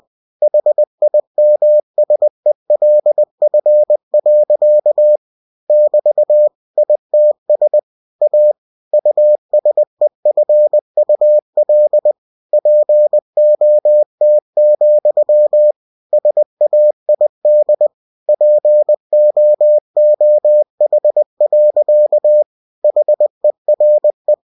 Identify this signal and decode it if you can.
Morse code, heard as a keyed tone, about 20 wpm, 600 Hz. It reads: HIMSELF. <BT> ITS A USEFUL POT, SAID POOH. HERE
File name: cw-books / winnie-the-pooh / winnie-20wpm-page-0057